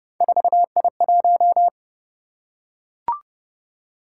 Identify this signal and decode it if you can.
Morse code, heard as a keyed tone, about 29 wpm, 700 Hz. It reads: 4I1